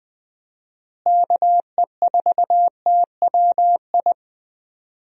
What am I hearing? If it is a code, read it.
Morse code, heard as a keyed tone, about 20 wpm, 700 Hz. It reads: KE4TWI
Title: USA Call Signs - 6 Characters - Repeated 3x - 20wpm